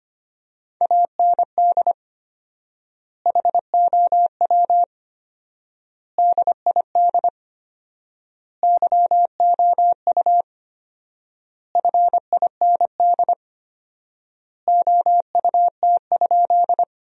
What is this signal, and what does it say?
Morse code, heard as a keyed tone, about 25 wpm, 700 Hz. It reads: AND HOW DID YOU FIND OUT?
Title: 4x Word-Spacing - Sentences from Top 100 Words - 25wpm